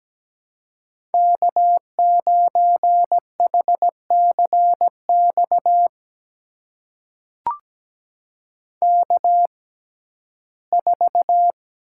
Morse code, heard as a keyed tone, about 17 wpm, spaced as for 12 wpm, 700 Hz. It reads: K9HCX K4
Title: USA Call Signs - 5 Characters - 6x Character Spacing - 17wpm